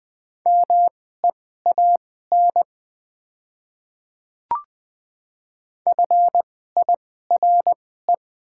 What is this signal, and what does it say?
Morse code, heard as a keyed tone, about 20 wpm, spaced as for 15 wpm, 700 Hz. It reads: MEAN FIRE
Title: 2x Character Spacing - Top 500 Words - 20wpm